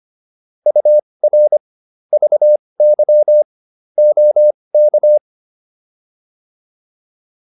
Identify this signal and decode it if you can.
Morse code, heard as a keyed tone, about 25 wpm, spaced as for 20 wpm, 600 Hz. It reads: UR VY OK